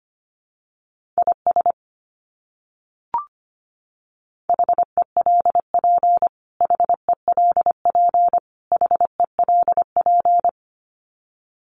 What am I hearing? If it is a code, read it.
Morse code, heard as a keyed tone, about 25 wpm, 700 Hz. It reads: IS HELP HELP HELP
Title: Top 200 Words Repeated 25wpm